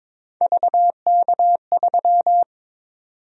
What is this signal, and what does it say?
Morse code, heard as a keyed tone, about 22 wpm, 700 Hz. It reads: VK3
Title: USA Call Sign Prefixes - Repeated 2x - 22wpm